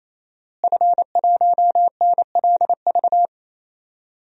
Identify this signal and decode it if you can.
Morse code, heard as a keyed tone, about 28 wpm, 700 Hz. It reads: F1NLV
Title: DXCC-100 Call Signs - 5 Characters - Repeated 3x - 28wpm